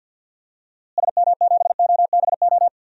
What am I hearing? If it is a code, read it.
Morse code, heard as a keyed tone, about 52 wpm, 700 Hz. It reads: SM7OBO
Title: DXCC-100 Call Signs - 6 Characters - Repeated 3x - 50wpm